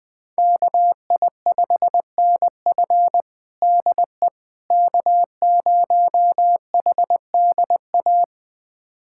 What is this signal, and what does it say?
Morse code, heard as a keyed tone, about 20 wpm, 700 Hz. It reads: KI5NF DE K0HDA
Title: USA Callsign DE Callsign - 5 Characters - 20wpm